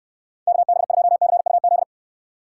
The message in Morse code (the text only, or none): KB2CUZ